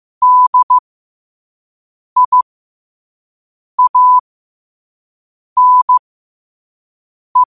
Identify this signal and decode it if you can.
Morse code, heard as a keyed tone, about 15 wpm, spaced as for 5 wpm, 1000 Hz. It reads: DIANE
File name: qotdfiles / 5wpm